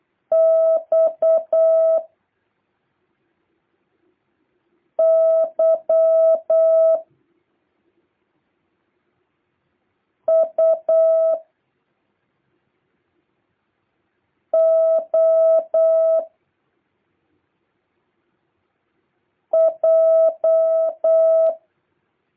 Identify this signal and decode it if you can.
Morse code, heard as a keyed tone, about 8 wpm, 650 Hz. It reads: X Y U O J
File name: jh8cbh / 2023cw